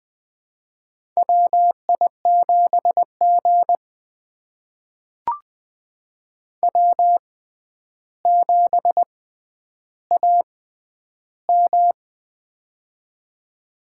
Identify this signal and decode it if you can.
Morse code, heard as a keyed tone, about 20 wpm, spaced as for 10 wpm, 700 Hz. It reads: WI7G W7AM
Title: USA Call Signs - 4 Characters - 6x Character Spacing - 20wpm